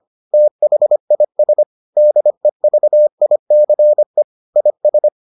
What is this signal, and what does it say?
Morse code, heard as a keyed tone, about 25 wpm, 600 Hz. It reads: THIS DEVICE IS